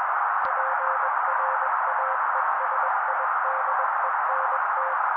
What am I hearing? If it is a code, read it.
Morse code, heard as a keyed tone, about 21 wpm, 530 Hz. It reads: PRAESIDENT
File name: daily / 20wpm noise